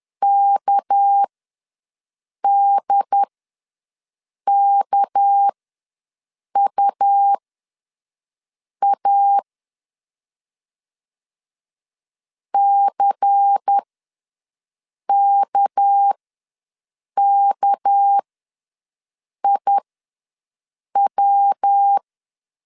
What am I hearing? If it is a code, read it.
Morse code, heard as a keyed tone, about 10 wpm, spaced as for 5 wpm, 800 Hz. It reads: KDKUA CKKIW
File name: LEZIONI / 6 PKX0 / 6 PK ws 10wpm 3745